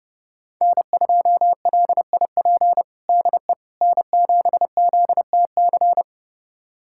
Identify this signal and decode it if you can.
Morse code, heard as a keyed tone, about 29 wpm, 700 Hz. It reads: N2LIP DE N7ZTC
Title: USA Callsign DE Callsign - 5 Characters - 30wpm